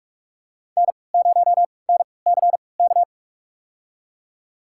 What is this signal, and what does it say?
Morse code, heard as a keyed tone, about 44 wpm, spaced as for 25 wpm, 700 Hz. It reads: N0NCK
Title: USA Call Signs - 5 Characters - 3x Character Spacing - 45wpm